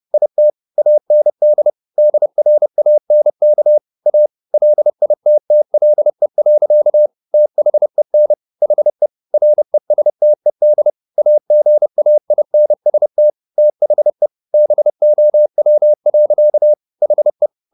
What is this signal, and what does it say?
Morse code, heard as a keyed tone, about 30 wpm, 600 Hz. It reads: IT AND DRANK A LITTLE. THEN HE RESTED AGAINST THE BOW. HE
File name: the old man and the sea / the old man and the sea 1 30wpm